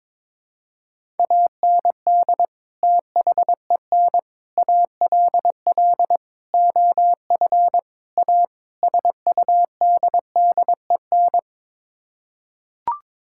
Morse code, heard as a keyed tone, about 22 wpm, 700 Hz. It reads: AND THEN ALL OF A SUDDEN
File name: practice / 22wpm / Sets of 6 Words - 22wpm